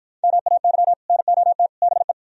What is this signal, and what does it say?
Morse code, heard as a keyed tone, about 52 wpm, 700 Hz. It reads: MAY NOT BE